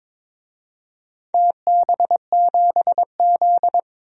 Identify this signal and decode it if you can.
Morse code, heard as a keyed tone, about 22 wpm, 700 Hz. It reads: TB7Z
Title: DXCC-100 Call Signs - 4 Characters - Repeated 2x - 22wpm